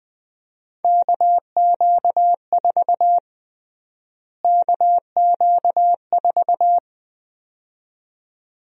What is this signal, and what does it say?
Morse code, heard as a keyed tone, about 20 wpm, 700 Hz. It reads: KQ4 KQ4
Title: USA Call Sign Prefixes - Repeated 2x - 20wpm